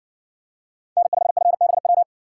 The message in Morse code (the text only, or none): N5FBW